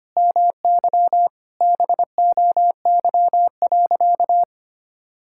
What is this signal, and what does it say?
Morse code, heard as a keyed tone, about 25 wpm, 700 Hz. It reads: MY BOY.
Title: Sentences from Top 1000 Words - Review Part 5 - 25wpm